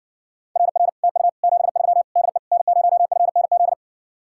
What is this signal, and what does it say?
Morse code, heard as a keyed tone, about 60 wpm, 700 Hz. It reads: RR TU 73 DE N0FNZ